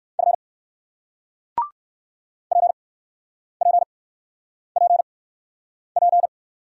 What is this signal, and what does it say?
Morse code, heard as a keyed tone, about 60 wpm, spaced as for 9 wpm, 700 Hz. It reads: F PPPP